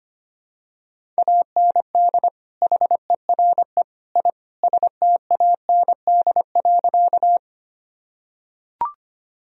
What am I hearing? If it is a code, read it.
Morse code, heard as a keyed tone, about 25 wpm, 700 Hz. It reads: AND HERE I STAND.
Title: Sentences from Top 1000 Words - Review Part 1 - 25wpm